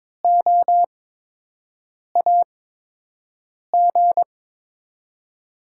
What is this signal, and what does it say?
Morse code, heard as a keyed tone, about 23 wpm, spaced as for 6 wpm, 700 Hz. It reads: OAG
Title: USA Call Signs - 6 Characters - 8x Character Spacing - 22wpm